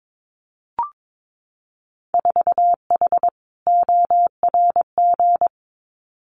Morse code, heard as a keyed tone, about 22 wpm, 700 Hz. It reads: 4H ORG
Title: Winter Field Day Exchanges Encore 22wpm